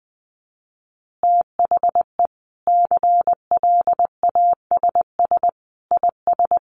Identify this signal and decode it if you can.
Morse code, heard as a keyed tone, about 20 wpm, 700 Hz. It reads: THE CLASS IS
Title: Additional Easy Phrases 20wpm